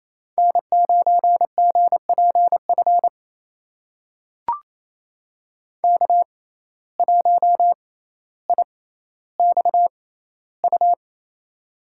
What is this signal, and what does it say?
Morse code, heard as a keyed tone, about 28 wpm, spaced as for 14 wpm, 700 Hz. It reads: N9GPF K1IXU
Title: USA Call Signs - 5 Characters - 6x Character Spacing - 28wpm